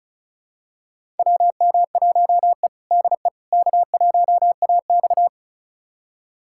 Morse code, heard as a keyed tone, about 36 wpm, 700 Hz. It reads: WM1E DE K1AX
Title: USA Callsign DE Callsign - 4 Characters - 35wpm